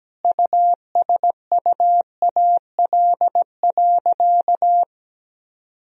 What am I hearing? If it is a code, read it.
Morse code, heard as a keyed tone, about 17 wpm, 700 Hz. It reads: USUAL.
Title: Sentences from Top 1000 Words - Review Part 2 - 17wpm